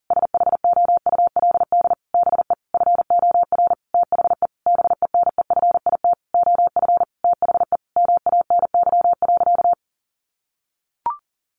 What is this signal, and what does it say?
Morse code, heard as a keyed tone, about 40 wpm, 700 Hz. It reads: SHOULD BE FOR THE BENEFIT OF THE MANY.
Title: Sentences from Top 1300 Words 40wpm